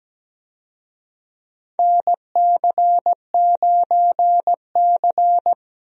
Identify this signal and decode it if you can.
Morse code, heard as a keyed tone, about 17 wpm, 700 Hz. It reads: NC9C
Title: USA Call Signs - 4 Characters - Repeated 3x - 17wpm